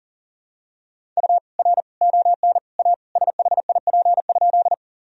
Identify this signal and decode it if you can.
Morse code, heard as a keyed tone, about 40 wpm, 700 Hz. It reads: U R ON A SHIP?
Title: Sentences from Top 400 Words and Abbreviations 40wpm